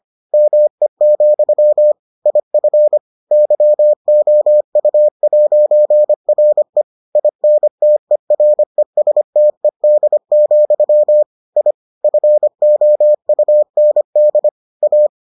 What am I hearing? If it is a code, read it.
Morse code, heard as a keyed tone, about 25 wpm, 600 Hz. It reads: ME, IF YOU'RE INTERESTED, I FOUND A